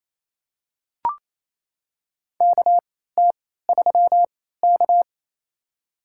Morse code, heard as a keyed tone, about 28 wpm, spaced as for 16 wpm, 700 Hz. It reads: KT3K
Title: USA Call Signs - 4 Characters - 3x Character Spacing - 28wpm